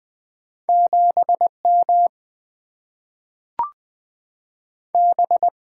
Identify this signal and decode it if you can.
Morse code, heard as a keyed tone, about 20 wpm, 700 Hz. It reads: 7M B
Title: Single-Letter Single-Number 20wpm